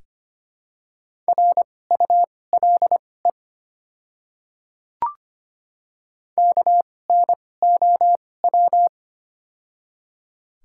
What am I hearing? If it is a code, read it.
Morse code, heard as a keyed tone, about 25 wpm, spaced as for 18 wpm, 700 Hz. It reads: RULE KNOW
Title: 2x Character Spacing - Top 500 Words - 25wpm